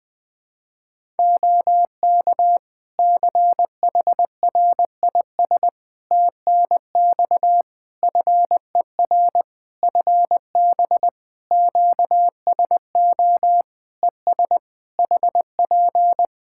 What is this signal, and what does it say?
Morse code, heard as a keyed tone, about 20 wpm, 700 Hz. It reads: OK CHRIS TNX FER FB QSO ES HP
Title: Practice QSO - Exchange 4 - Ending - 20wpm